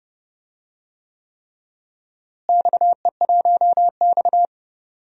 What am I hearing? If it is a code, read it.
Morse code, heard as a keyed tone, about 32 wpm, 700 Hz. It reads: XE1X
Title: DXCC-100 Call Signs - 4 Characters - Repeated 3x - 30wpm